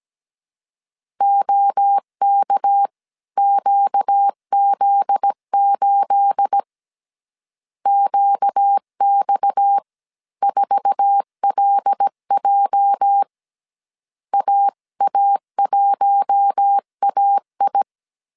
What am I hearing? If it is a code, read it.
Morse code, heard as a keyed tone, about 17 wpm, spaced as for 15 wpm, 800 Hz. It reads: OKQZ8 QX4LJ AA1AI
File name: CW / DET / 1 9 1500C S C 16wpm 2909